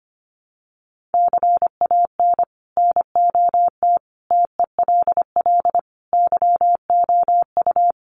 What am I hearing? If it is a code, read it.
Morse code, heard as a keyed tone, about 25 wpm, 700 Hz. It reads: CAN NOT TELL YOU
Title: Sets of 5 Words Encore2 25wpm